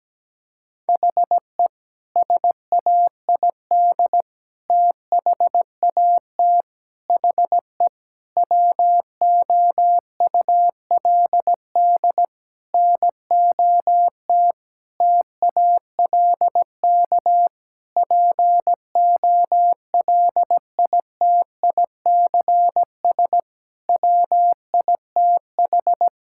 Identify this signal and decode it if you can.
Morse code, heard as a keyed tone, about 17 wpm, 700 Hz. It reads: HE SAID THAT HE WOULD NOT TALK POLITICS WITH
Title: Sentences from Top 1000 Words - Review Part 6 - 17wpm